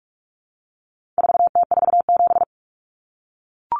Morse code, heard as a keyed tone, about 44 wpm, 700 Hz. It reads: 4T47
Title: Common Cut Numbers - 4-Digit 45wpm